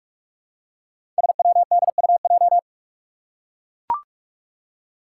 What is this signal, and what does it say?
Morse code, heard as a keyed tone, about 43 wpm, 700 Hz. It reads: SWDUJ
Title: Five Challenging Letter-Number Combinations 45wpm